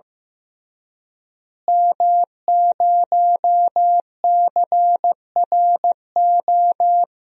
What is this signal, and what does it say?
Morse code, heard as a keyed tone, about 15 wpm, 700 Hz. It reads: M0CRO
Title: DXCC-100 Call Signs - 5 Characters - Repeated 2x - 15wpm